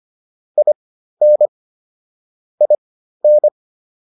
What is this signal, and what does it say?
Morse code, heard as a keyed tone, about 25 wpm, spaced as for 13 wpm, 600 Hz. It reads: IN IN